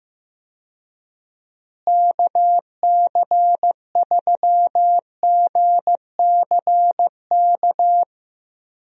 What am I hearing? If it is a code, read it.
Morse code, heard as a keyed tone, about 15 wpm, 700 Hz. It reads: KC3GCK
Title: USA Call Signs - 6 Characters - Repeated 3x - 15wpm